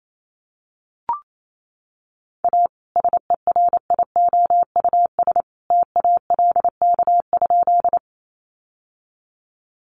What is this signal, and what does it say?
Morse code, heard as a keyed tone, about 28 wpm, 700 Hz. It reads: A SERIOUS TALK?
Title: Sentences from Top 1100 Words 28wpm